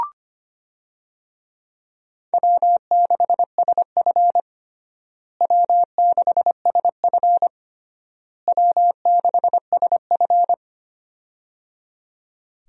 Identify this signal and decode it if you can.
Morse code, heard as a keyed tone, about 25 wpm, 700 Hz. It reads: W6SF W6SF W6SF
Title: USA Call Signs - 4 Characters - Repeated 3x - 25wpm